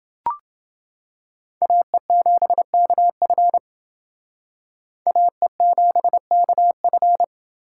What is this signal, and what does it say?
Morse code, heard as a keyed tone, about 28 wpm, 700 Hz. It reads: AE7KF AE7KF